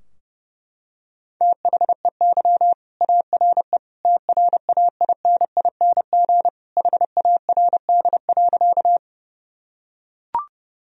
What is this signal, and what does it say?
Morse code, heard as a keyed tone, about 30 wpm, 700 Hz. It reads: THEY ARE TRAINING HARD.
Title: Sentences from Top 1000 Words - Review Part 4 - 30wpm